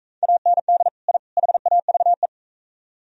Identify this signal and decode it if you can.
Morse code, heard as a keyed tone, about 39 wpm, 700 Hz. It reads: AND I HAVE